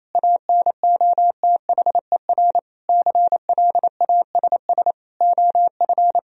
Morse code, heard as a keyed tone, about 28 wpm, 700 Hz. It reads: ANOTHER CLASS OF
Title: Sentences from Top 1000 Words - Review Part 2 - 28wpm